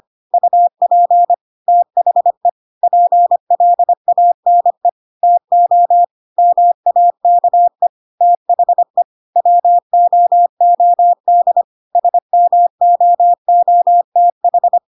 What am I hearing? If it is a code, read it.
Morse code, heard as a keyed tone, about 25 wpm, 700 Hz. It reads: UP THE PLANE TO MAKE THE WOOD SMOOTH
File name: Pinocchio / Pinocchio ch1 25 25 ik1hge